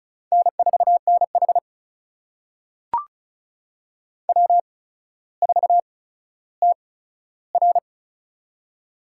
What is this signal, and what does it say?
Morse code, heard as a keyed tone, about 34 wpm, spaced as for 15 wpm, 700 Hz. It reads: N4NH W4TR
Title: USA Call Signs - 4 Characters - 8x Character Spacing - 35wpm